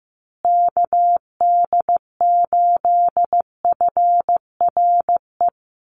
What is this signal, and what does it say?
Morse code, heard as a keyed tone, about 15 wpm, 700 Hz. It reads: KD8FRE
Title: USA Call Signs Repeated 2x 15wpm